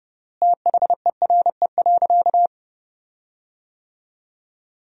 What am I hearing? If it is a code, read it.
Morse code, heard as a keyed tone, about 32 wpm, 700 Hz. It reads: THERE.
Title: 3x Word-Spacing - Sentences from Top 500 Words - 30wpm